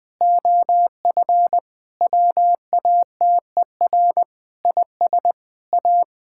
Morse code, heard as a keyed tone, about 20 wpm, 700 Hz. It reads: OF WATER IS A
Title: Sentences with 3-Word Phrases - Spoken Beginning - 20wpm